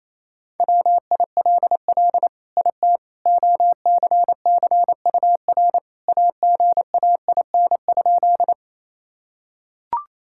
Morse code, heard as a keyed tone, about 28 wpm, 700 Hz. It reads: WILL IT OCCUR AGAIN?
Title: Sentences from Top 1000 Words - Review Part 6 - 28wpm